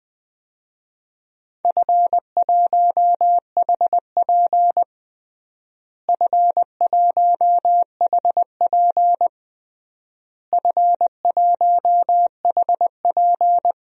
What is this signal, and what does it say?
Morse code, heard as a keyed tone, about 21 wpm, 700 Hz. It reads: F1HP F1HP F1HP
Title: DXCC-100 Call Signs - 4 Characters - Repeated 3x - 20wpm